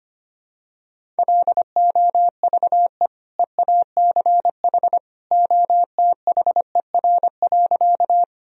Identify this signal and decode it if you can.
Morse code, heard as a keyed tone, about 25 wpm, 700 Hz. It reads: LOVE EACH OTHER.
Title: Sentences from Top 1000 Words - Review Part 1 - 25wpm